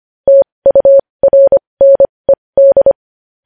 Morse code, heard as a keyed tone, about 25 wpm, spaced as for 20 wpm, 550 Hz. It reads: TURNED
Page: news-headlines-in-morse-code-at-20-wpm